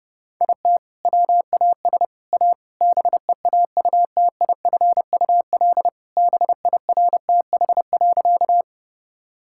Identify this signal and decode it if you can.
Morse code, heard as a keyed tone, about 28 wpm, 700 Hz. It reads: IT WAS A BEAUTIFUL BIRTH.
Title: Sentences from Top 1000 Words - Review Part 5 - 30wpm